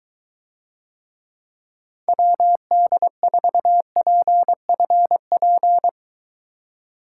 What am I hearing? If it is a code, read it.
Morse code, heard as a keyed tone, about 21 wpm, 700 Hz. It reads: WD4PFP